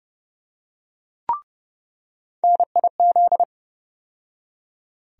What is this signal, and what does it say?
Morse code, heard as a keyed tone, about 30 wpm, 700 Hz. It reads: NIZ
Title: USA Call Sign Postfixes - 3 Letters 30wpm